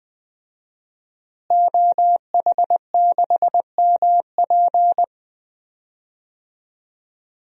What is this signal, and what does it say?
Morse code, heard as a keyed tone, about 20 wpm, 700 Hz. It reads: OH6MP
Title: DXCC-100 Call Signs - 5 Characters - Repeated 3x - 20wpm